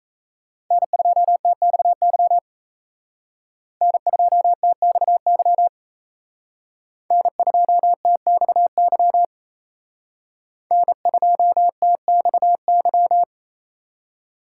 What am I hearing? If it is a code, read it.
Morse code, heard as a keyed tone, about 34 wpm, 700 Hz. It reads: N2TXY N2TXY N2TXY N2TXY
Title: Speed Racing - USA 1x3 Call Signs 28wpm